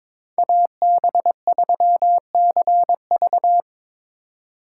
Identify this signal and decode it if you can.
Morse code, heard as a keyed tone, about 26 wpm, 700 Hz. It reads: AB3CV
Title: Speed Racing - USA 2x2 Call Signs 22wpm